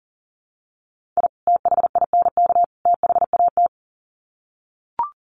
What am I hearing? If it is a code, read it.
Morse code, heard as a keyed tone, about 40 wpm, 700 Hz. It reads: I THINK THAT
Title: Sets of 3 Words 40wpm